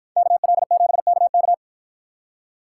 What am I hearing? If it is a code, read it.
Morse code, heard as a keyed tone, about 53 wpm, 700 Hz. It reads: KL7KK